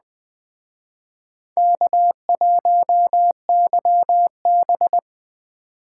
Morse code, heard as a keyed tone, about 20 wpm, 700 Hz. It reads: K1YB
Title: Speed Racing - USA Call Signs 17wpm